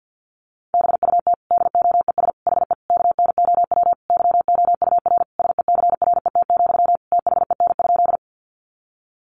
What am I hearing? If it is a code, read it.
Morse code, heard as a keyed tone, about 50 wpm, 700 Hz. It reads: BUT DOES HE KNOW YOUR SECRET, THEN?
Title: Sentences from Top 1200 Words 50wpm